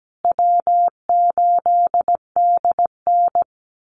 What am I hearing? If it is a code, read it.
Morse code, heard as a keyed tone, about 17 wpm, 700 Hz. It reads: W8DN
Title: Call Signs Repeated 2x - 17wpm